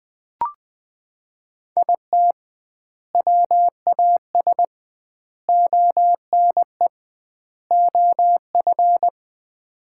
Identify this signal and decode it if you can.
Morse code, heard as a keyed tone, about 20 wpm, 700 Hz. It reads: IT WAS ONE OF
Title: 2x Word-Spacing - Sets of 4 Words - 20wpm